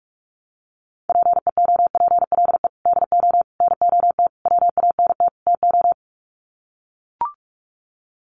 Morse code, heard as a keyed tone, about 44 wpm, 690 Hz. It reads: PEOPLE DO NOT WANT TO
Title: Sets of 5 Words Encore2 45wpm